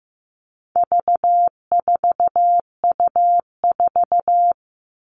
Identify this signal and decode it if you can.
Morse code, heard as a keyed tone, about 15 wpm, 700 Hz. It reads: V4U4
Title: All Cut Numbers - 4-Digit 15wpm